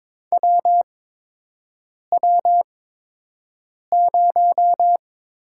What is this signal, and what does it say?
Morse code, heard as a keyed tone, about 22 wpm, spaced as for 6 wpm, 700 Hz. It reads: WW0
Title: USA Call Signs - 8x Character Spacing - 22wpm